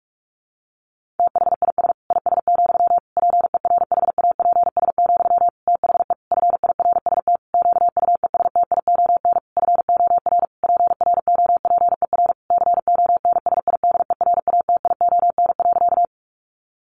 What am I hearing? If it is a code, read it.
Morse code, heard as a keyed tone, about 44 wpm, 700 Hz. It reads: THIS IS, PERHAPS, THE FIRST QUESTION FOR PROPER CONSIDERATION.
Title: Sentences from Top 1500 Words 45wpm